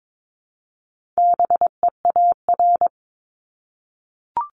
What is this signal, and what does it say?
Morse code, heard as a keyed tone, about 22 wpm, 700 Hz. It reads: BEAR
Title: ING Suffix - Top 100 Words with Base Words 22wpm